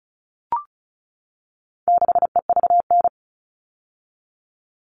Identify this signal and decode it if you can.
Morse code, heard as a keyed tone, about 34 wpm, 700 Hz. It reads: 6EVN
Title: All Cut Numbers - 4-Digit 35wpm